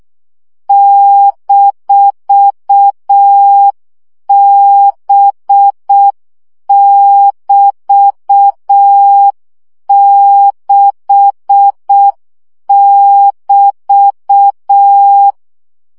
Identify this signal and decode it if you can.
Morse code, heard as a keyed tone, about 6 wpm, 800 Hz. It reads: -B<BT>6<BT>
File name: CW gyakorlas b 6 egyenlo minusz 5 baud